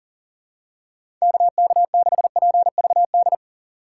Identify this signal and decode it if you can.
Morse code, heard as a keyed tone, about 40 wpm, 700 Hz. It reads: KK6PVD
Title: USA Call Signs - 6 Characters - 2x Character Spacing - 40wpm